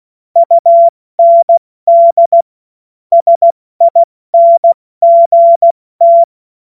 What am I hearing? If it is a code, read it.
Morse code, heard as a keyed tone, about 16 wpm, 680 Hz. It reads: UND SINGT
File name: Morsekurs / Witze / Witz 16. November 2021, 70 F80